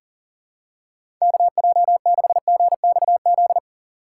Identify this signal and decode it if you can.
Morse code, heard as a keyed tone, about 40 wpm, 700 Hz. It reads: KJ6GXZ
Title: USA Call Signs - 6 Characters - 2x Character Spacing - 40wpm